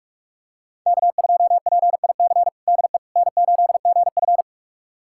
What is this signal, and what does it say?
Morse code, heard as a keyed tone, about 47 wpm, 700 Hz. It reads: K2PIC DE N8GF
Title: USA Callsign DE Callsign 45wpm